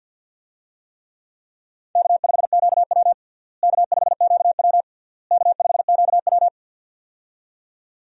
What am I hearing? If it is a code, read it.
Morse code, heard as a keyed tone, about 51 wpm, 700 Hz. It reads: K5QW K5QW K5QW
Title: USA Call Signs - 4 Characters - Repeated 3x - 50wpm